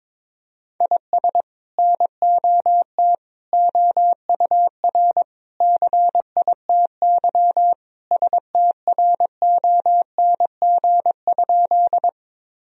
Morse code, heard as a keyed tone, about 22 wpm, 700 Hz. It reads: IS NOT OUR CITY STRONG?
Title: Sentences from Top 1000 Words - Review Part 3 - 22wpm